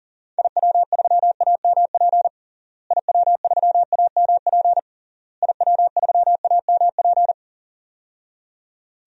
Morse code, heard as a keyed tone, about 40 wpm, 700 Hz. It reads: IW3AMP IW3AMP IW3AMP
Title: DXCC-100 Call Signs - 6 Characters - Repeated 3x - 40wpm